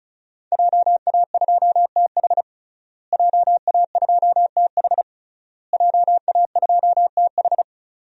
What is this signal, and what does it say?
Morse code, heard as a keyed tone, about 34 wpm, 700 Hz. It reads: JA2TH JA2TH JA2TH
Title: DXCC-100 Call Signs - 5 Characters - Repeated 3x - 35wpm